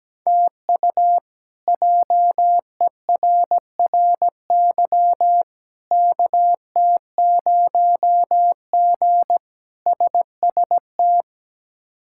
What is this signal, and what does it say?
Morse code, heard as a keyed tone, about 17 wpm, 700 Hz. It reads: TU JERRY KT0G SST
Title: K1USN SST QSO Practice - 17wpm